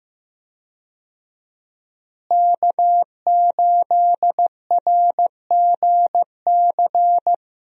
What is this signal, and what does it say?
Morse code, heard as a keyed tone, about 15 wpm, 700 Hz. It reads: K8RGC
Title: USA Call Signs - 5 Characters - Repeated 2x - 15wpm